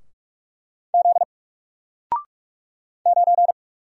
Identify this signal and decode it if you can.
Morse code, heard as a keyed tone, about 44 wpm, 700 Hz. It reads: Z 9
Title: Single Letter-Number 45wpm